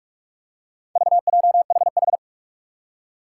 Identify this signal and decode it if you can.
Morse code, heard as a keyed tone, about 46 wpm, 700 Hz. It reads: VJHH
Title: Four Challenging Letter-Number Combinations 45wpm